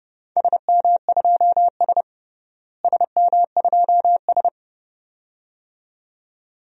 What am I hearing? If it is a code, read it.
Morse code, heard as a keyed tone, about 30 wpm, 700 Hz. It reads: SM2S SM2S
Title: DXCC-100 Call Signs - 4 Characters - Repeated 2x - 30wpm